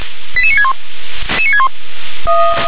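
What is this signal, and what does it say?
Five-tone selective call (ZVEI), heard as quick runs of five tones, tones of about 70 ms. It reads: 88961
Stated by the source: ZVEI five-tone call received on SDR.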